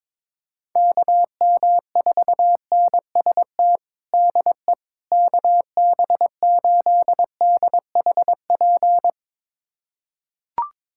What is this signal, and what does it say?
Morse code, heard as a keyed tone, about 22 wpm, 700 Hz. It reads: KM4NST DE KB8DHP
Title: USA Callsign DE Callsign 22wpm